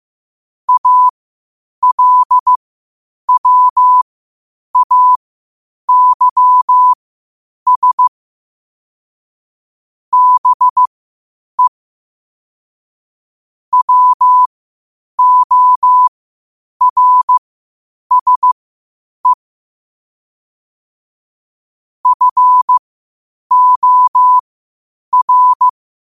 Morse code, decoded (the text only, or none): ALWAYS BE WORSE FOR